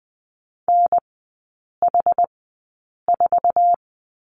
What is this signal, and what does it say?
Morse code, heard as a keyed tone, about 20 wpm, spaced as for 8 wpm, 700 Hz. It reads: NH4
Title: 060 Lesson 11e - Letter h - Three Characters 20wpm